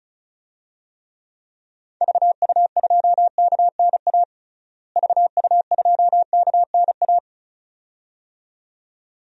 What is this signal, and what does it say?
Morse code, heard as a keyed tone, about 36 wpm, 700 Hz. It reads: VU2KNA VU2KNA
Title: DXCC-100 Call Signs - 6 Characters - Repeated 2x - 35wpm